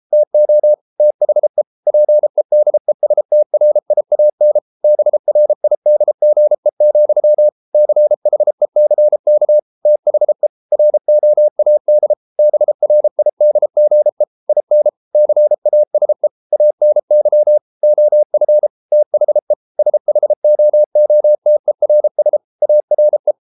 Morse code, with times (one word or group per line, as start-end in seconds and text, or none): 0.1-0.7 TO
1.0-1.6 THE
1.9-4.6 PEDESTRIAN
4.8-7.5 BRIDGE,
7.7-9.6 CHECK
9.9-10.5 THE
10.7-12.1 ROAD
12.4-14.2 BRIDGE
14.5-14.9 IN
15.2-16.3 CASE
16.5-17.6 ANY
17.8-18.7 OF
18.9-19.5 THE
19.8-22.4 SHOOTERS
22.6-23.3 ARE